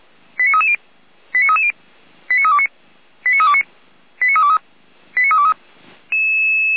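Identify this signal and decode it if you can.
Five-tone selective call (ZVEI), heard as quick runs of five tones, tones of about 70 ms. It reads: 89330 89330 89329 89329 89323 89323
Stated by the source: ZVEI five-tone selective call 89508 received by SDR, logged by feuerwehr-orlamuende fire brigade.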